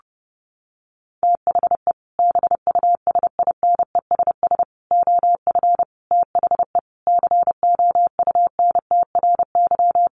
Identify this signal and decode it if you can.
Morse code, heard as a keyed tone, about 30 wpm, 700 Hz. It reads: THE BUSINESS OF THE COUNTRY
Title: Sentences from Top 1500 Words 30wpm